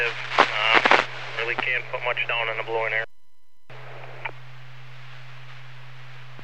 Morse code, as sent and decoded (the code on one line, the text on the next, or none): none